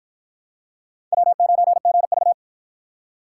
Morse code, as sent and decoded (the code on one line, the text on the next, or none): .-- ----. --. ...-
W9GV